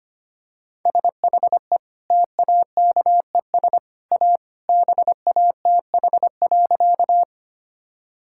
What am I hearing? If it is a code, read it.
Morse code, heard as a keyed tone, about 25 wpm, 690 Hz. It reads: SHE TAKES A BATH.
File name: Easy Sentences with Hints - 25wpm